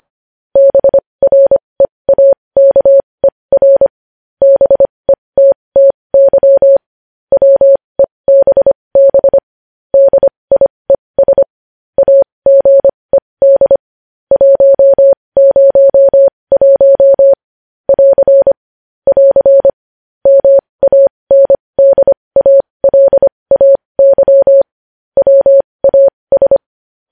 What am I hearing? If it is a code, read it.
Morse code, heard as a keyed tone, about 25 wpm, spaced as for 20 wpm, 550 Hz. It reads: BREAKER BETTY WEBB DIES AGED 101 <AR> <AR> MANDALAY WAS
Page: news-headlines-in-morse-code-at-20-wpm